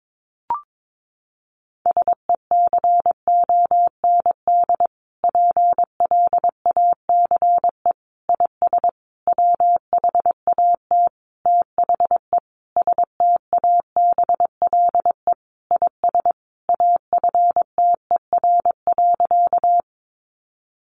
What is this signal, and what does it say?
Morse code, heard as a keyed tone, about 22 wpm, 700 Hz. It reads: SECOND PLACE IS WHAT THE STABLE IS AFTER.
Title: Sentences from Top 1500 Words 22wpm